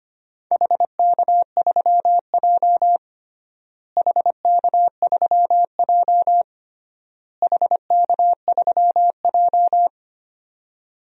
Morse code, heard as a keyed tone, about 25 wpm, 700 Hz. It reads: HK3J HK3J HK3J
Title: DXCC-100 Call Signs - 4 Characters - Repeated 3x - 25wpm